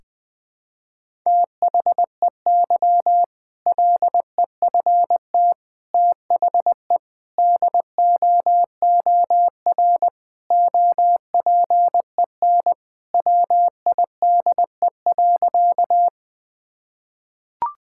Morse code, heard as a keyed tone, about 21 wpm, 700 Hz. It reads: THEY LEFT THE DOOR OPEN WIDE.
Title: Sentences from Top 1000 Words - Review Part 6 - 20wpm